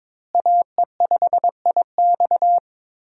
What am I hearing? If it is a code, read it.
Morse code, heard as a keyed tone, about 22 wpm, 700 Hz. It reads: AE5IX
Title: USA Call Signs - 5 Characters - Repeated 3x - 22wpm